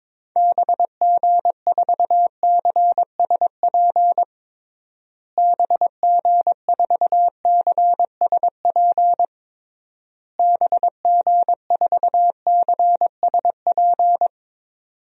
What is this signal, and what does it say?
Morse code, heard as a keyed tone, about 22 wpm, 700 Hz. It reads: BG4CSP BG4CSP BG4CSP
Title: DXCC-100 Call Signs - 6 Characters - Repeated 3x - 22wpm